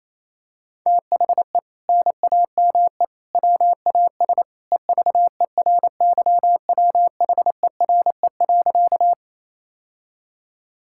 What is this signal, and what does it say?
Morse code, heard as a keyed tone, about 28 wpm, 700 Hz. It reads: THE NAME WAS EVERYWHERE.
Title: Sentences from Top 1600 Words 28wpm